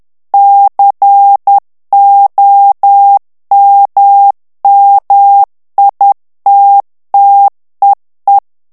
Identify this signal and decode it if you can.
Morse code, heard as a keyed tone, about 10 wpm, 800 Hz. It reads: COMMITTEE
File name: ANGLAIS texte clair 3 vitesse 10 mots minute